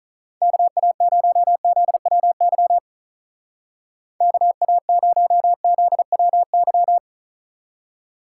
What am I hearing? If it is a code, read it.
Morse code, heard as a keyed tone, about 44 wpm, 700 Hz. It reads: KA0ZWY KA0ZWY